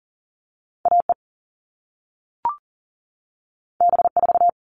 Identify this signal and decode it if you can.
Morse code, heard as a keyed tone, about 38 wpm, 700 Hz. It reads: AE B4
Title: All Cut Numbers - 2-Digit 40wpm